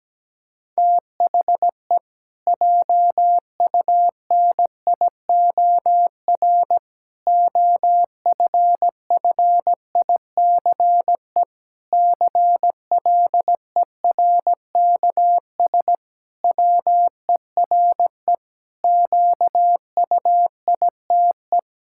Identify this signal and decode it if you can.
Morse code, heard as a keyed tone, about 17 wpm, 700 Hz. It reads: THE JUNIOR OFFICE CLERKS WERE QUITE
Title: Pangram Challenge v2 - 17wpm